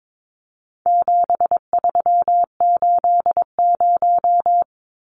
Morse code, heard as a keyed tone, about 22 wpm, 700 Hz. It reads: 7380
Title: Numbers - 4-Digit 22wpm